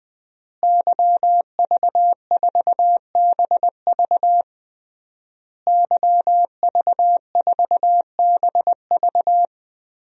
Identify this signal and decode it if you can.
Morse code, heard as a keyed tone, about 20 wpm, 700 Hz. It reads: YV4BV YV4BV
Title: DXCC-100 Call Signs - 5 Characters - Repeated 3x - 20wpm